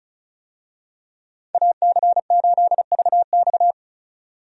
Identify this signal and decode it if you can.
Morse code, heard as a keyed tone, about 34 wpm, 700 Hz. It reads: AC8VX
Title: USA Call Signs - 5 Characters - 35wpm